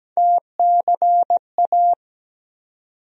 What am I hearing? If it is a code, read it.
Morse code, heard as a keyed tone, about 17 wpm, 700 Hz. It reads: TCA
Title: USA Call Sign Postfixes - 3 Letters 17wpm